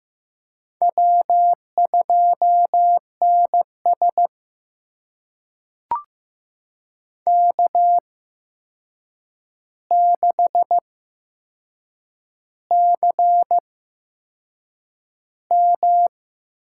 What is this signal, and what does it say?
Morse code, heard as a keyed tone, about 15 wpm, spaced as for 6 wpm, 700 Hz. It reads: W2NS K6CM
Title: USA Call Signs - 4 Characters - 8x Character Spacing - 15wpm